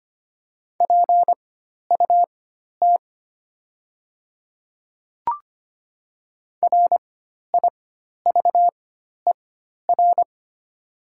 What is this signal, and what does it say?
Morse code, heard as a keyed tone, about 25 wpm, spaced as for 12 wpm, 700 Hz. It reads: PUT RIVER
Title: 4x Character Spacing - Top 300 Words - 25wpm